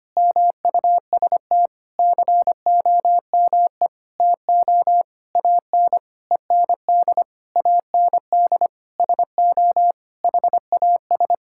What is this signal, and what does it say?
Morse code, heard as a keyed tone, about 25 wpm, 700 Hz. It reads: MUST COME TO AN END AND SO HAS
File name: practice / 25wpm / Sentences with 9-Word Phrases - 25wpm